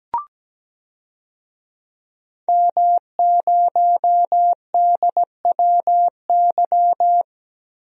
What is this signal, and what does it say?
Morse code, heard as a keyed tone, about 17 wpm, 700 Hz. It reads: M0DWY
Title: DXCC-100 Call Signs - 5 Characters - 17wpm